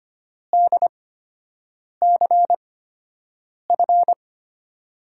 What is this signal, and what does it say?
Morse code, heard as a keyed tone, about 25 wpm, spaced as for 7 wpm, 700 Hz. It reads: DCF
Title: USA Call Signs - 6 Characters - 8x Character Spacing - 25wpm